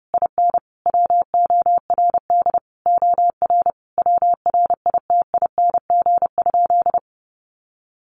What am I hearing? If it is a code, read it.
Morse code, heard as a keyed tone, about 30 wpm, 700 Hz. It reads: IN WORD OR WRITING?
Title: Sentences from Top 1200 Words 30wpm